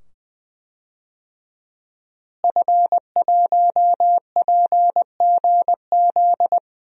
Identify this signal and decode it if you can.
Morse code, heard as a keyed tone, about 20 wpm, 700 Hz. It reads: F1PGZ
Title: DXCC-100 Call Signs - 5 Characters - Repeated 2x - 20wpm